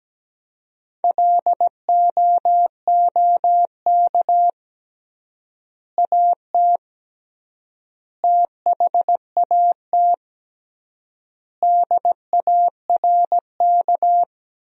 Morse code, heard as a keyed tone, about 17 wpm, 700 Hz. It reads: LOOK AT THAT DARK
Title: 3x Word-Spacing - Sentences from Top 500 Words - 17wpm